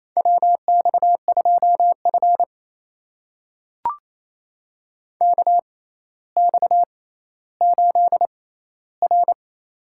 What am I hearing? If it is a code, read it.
Morse code, heard as a keyed tone, about 28 wpm, spaced as for 14 wpm, 700 Hz. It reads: WX2F KX8R
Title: USA Call Signs - 4 Characters - 6x Character Spacing - 28wpm